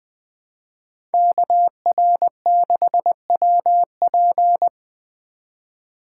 Morse code, heard as a keyed tone, about 20 wpm, 700 Hz. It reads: KR6WP
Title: Speed Racing - USA 2x2 Call Signs 17wpm